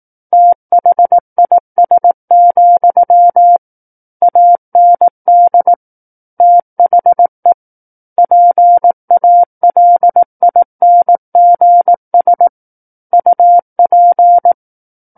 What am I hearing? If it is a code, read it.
Morse code, heard as a keyed tone, about 18 wpm, 700 Hz. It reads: THIS, AND THE PALINGS UP
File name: Book One / 09-Chapter 9 - 18 WPM